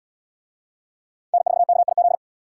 Morse code, heard as a keyed tone, about 69 wpm, 700 Hz. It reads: N4KIZ